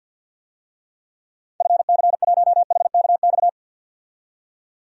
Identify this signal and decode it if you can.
Morse code, heard as a keyed tone, about 51 wpm, 700 Hz. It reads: FC1HKX
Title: DXCC-100 Call Signs - 6 Characters - 50wpm